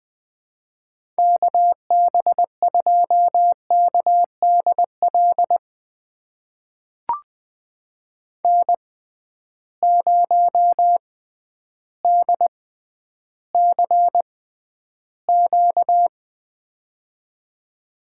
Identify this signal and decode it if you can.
Morse code, heard as a keyed tone, about 20 wpm, spaced as for 11 wpm, 700 Hz. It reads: KB2KDL N0DCQ
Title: USA Call Signs - 6x Character Spacing - 20wpm